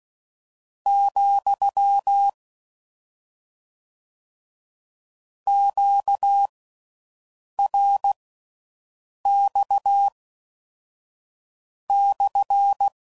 Morse code, heard as a keyed tone, about 16 wpm, spaced as for 6 wpm, 800 Hz. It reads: , QRX /